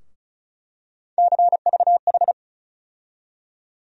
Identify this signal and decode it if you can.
Morse code, heard as a keyed tone, about 34 wpm, 700 Hz. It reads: CVH
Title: USA Call Sign Postfixes - 3 Letters 35wpm